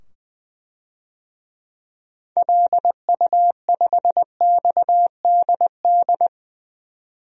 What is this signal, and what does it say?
Morse code, heard as a keyed tone, about 20 wpm, 700 Hz. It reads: LU5XDD
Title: DXCC-100 Call Signs - 6 Characters - 20wpm